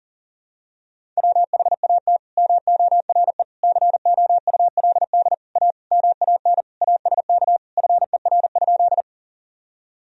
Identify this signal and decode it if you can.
Morse code, heard as a keyed tone, about 40 wpm, 700 Hz. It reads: WHAT MORE COULD A MAN ASK FER?
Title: Sentences from Top 200 Words and Abbreviations 40wpm